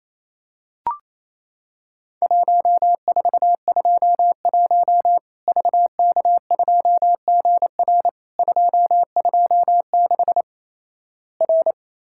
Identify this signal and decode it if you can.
Morse code, heard as a keyed tone, about 28 wpm, 700 Hz. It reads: 1421 VK2GR 226 R
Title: Worked All Europe DX Contest - QTC Exchange - 28wpm